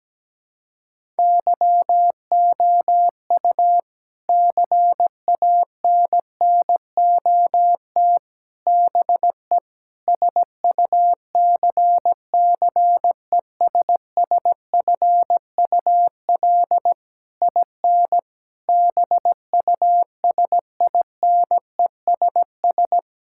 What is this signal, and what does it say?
Morse code, heard as a keyed tone, about 17 wpm, 700 Hz. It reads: YOU CANNOT BE SUCCESSFUL IN BUSINESS
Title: Sentences from Top 1000 Words - Review Part 6 - 17wpm